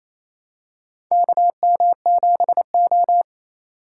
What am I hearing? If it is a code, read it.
Morse code, heard as a keyed tone, about 28 wpm, 700 Hz. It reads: KM7O
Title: USA Call Signs - 4 Characters - 4x Character Spacing - 28wpm